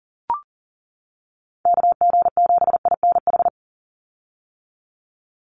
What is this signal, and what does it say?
Morse code, heard as a keyed tone, about 40 wpm, 700 Hz. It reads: KG7INH
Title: USA Call Signs 40wpm